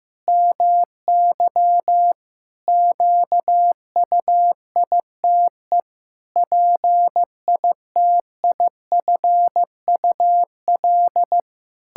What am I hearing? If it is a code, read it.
Morse code, heard as a keyed tone, about 15 wpm, 700 Hz. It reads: MY QUITE PITIFUL
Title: Pangram Challenge v2 - 15wpm